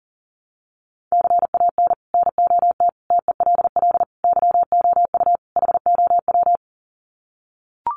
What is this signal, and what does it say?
Morse code, heard as a keyed tone, about 38 wpm, 700 Hz. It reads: CAN NOT TELL YOU HOW
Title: Sets of 5 Words Encore2 40wpm